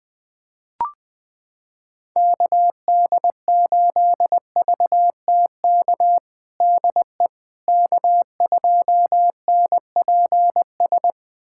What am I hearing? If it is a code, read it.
Morse code, heard as a keyed tone, about 20 wpm, 700 Hz. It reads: KD8VTK DE K2NPS
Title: USA Callsign DE Callsign 20wpm